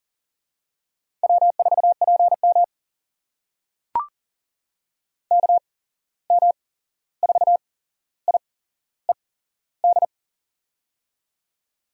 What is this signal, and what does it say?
Morse code, heard as a keyed tone, about 40 wpm, spaced as for 15 wpm, 700 Hz. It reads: W4PM KM4IED
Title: USA Call Signs - 8x Character Spacing - 40wpm